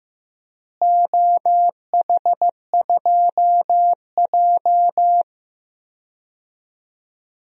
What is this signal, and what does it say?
Morse code, heard as a keyed tone, about 15 wpm, 700 Hz. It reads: OH2J
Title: DXCC-100 Call Signs - 4 Characters - Repeated 3x - 15wpm